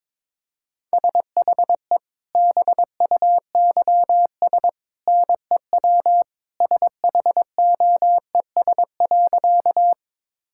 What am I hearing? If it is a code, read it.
Morse code, heard as a keyed tone, about 22 wpm, 700 Hz. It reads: SHE BUYS NEW SHOES.
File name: practice / 22wpm / Easy Sentences with Hints - 22wpm